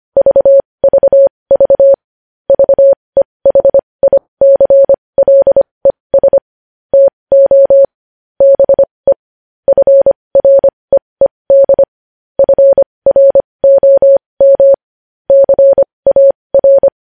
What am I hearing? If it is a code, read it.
Morse code, heard as a keyed tone, about 25 wpm, spaced as for 20 wpm, 550 Hz. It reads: VVV VEHICLES TO BE FREED FROM CAR
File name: podcast / News headlines-1x-25x20